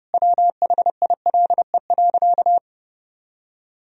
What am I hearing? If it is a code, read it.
Morse code, heard as a keyed tone, about 32 wpm, 700 Hz. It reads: WHILE.
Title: Sentences from Top 1000 Words - Review Part 1 - 30wpm